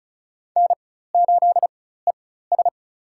Morse code, heard as a keyed tone, about 36 wpm, spaced as for 16 wpm, 700 Hz. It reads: N8ES
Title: USA Call Signs - 4 Characters - 4x Character Spacing - 35wpm